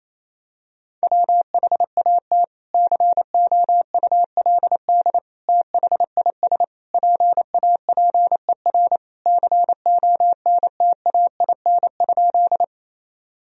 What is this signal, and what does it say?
Morse code, heard as a keyed tone, about 28 wpm, 700 Hz. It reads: WHAT COULD THIS PAPER CONTAIN?
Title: Sentences from Top 1000 Words - Review Part 2 - 28wpm